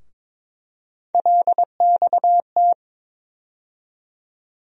Morse code, heard as a keyed tone, about 22 wpm, 700 Hz. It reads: LXT
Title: USA Call Sign Postfixes - 3 Letters 22wpm